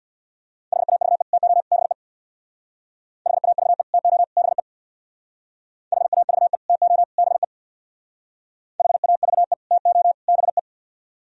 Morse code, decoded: HAVE TO BE HAVE TO BE HAVE TO BE HAVE TO BE

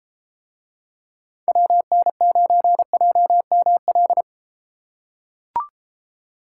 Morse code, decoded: WN9JML

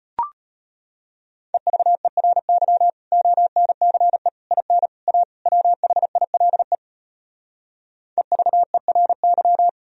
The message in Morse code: EVERY ONCE IN A WHILE EVERY

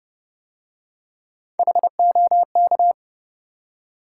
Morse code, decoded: HOK

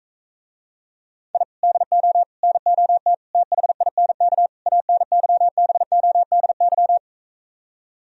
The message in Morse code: I DO NOT THINK ANYBODY

